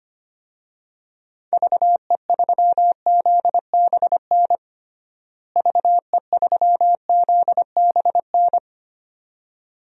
Morse code, heard as a keyed tone, about 25 wpm, 700 Hz. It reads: VE3ZBN VE3ZBN
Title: DXCC-100 Call Signs - 6 Characters - Repeated 2x - 25wpm